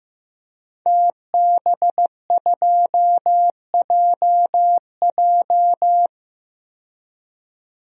Morse code, heard as a keyed tone, about 15 wpm, 700 Hz. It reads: TB2JJ
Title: DXCC-100 Call Signs - 5 Characters - 15wpm